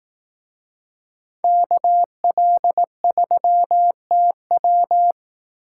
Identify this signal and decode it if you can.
Morse code, heard as a keyed tone, about 18 wpm, 700 Hz. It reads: KL3TW
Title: Speed Racing - USA 2x2 Call Signs 15wpm